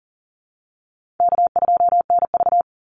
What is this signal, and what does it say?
Morse code, heard as a keyed tone, about 40 wpm, 700 Hz. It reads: K2NV
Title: Call Signs Repeated 2x - 40wpm